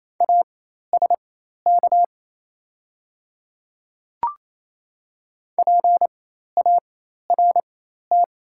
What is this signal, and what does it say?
Morse code, heard as a keyed tone, about 28 wpm, spaced as for 13 wpm, 690 Hz. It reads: ASK PART
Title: 4x Character Spacing - Top 200 Words - 28wpm